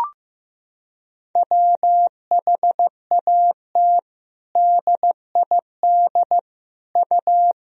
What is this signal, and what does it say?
Morse code, heard as a keyed tone, about 15 wpm, 700 Hz. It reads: WHAT DID U
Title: Sentences from Top 300 Words and Abbreviations 15wpm